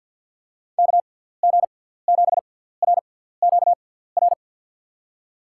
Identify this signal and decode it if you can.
Morse code, heard as a keyed tone, about 48 wpm, spaced as for 17 wpm, 700 Hz. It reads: KG7RQR
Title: USA Call Signs - 6 Characters - 6x Character Spacing - 50wpm